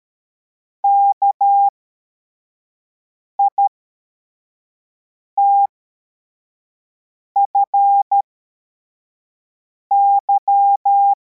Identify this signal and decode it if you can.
Morse code, heard as a keyed tone, about 13 wpm, spaced as for 4 wpm, 800 Hz. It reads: KITFY